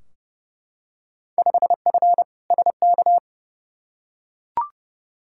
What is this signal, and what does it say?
Morse code, heard as a keyed tone, about 30 wpm, 700 Hz. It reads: 5F SK
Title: Field Day Exchanges Encore - 30wpm